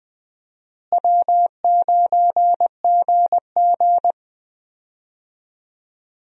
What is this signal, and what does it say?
Morse code, heard as a keyed tone, about 20 wpm, 700 Hz. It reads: W9GG
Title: USA Call Signs - 4 Characters - Repeated 3x - 20wpm